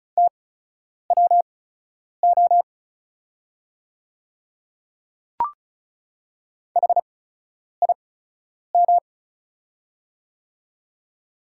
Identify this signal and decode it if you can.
Morse code, heard as a keyed tone, about 36 wpm, spaced as for 10 wpm, 690 Hz. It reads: TWO HIM
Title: 8x Character Spacing - Top 100 Words - 35wpm